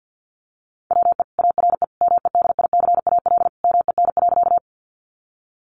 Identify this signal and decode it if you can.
Morse code, heard as a keyed tone, about 50 wpm, 700 Hz. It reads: WE ARE MEDICAL MEN.
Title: Sentences from Top 800 Words 50wpm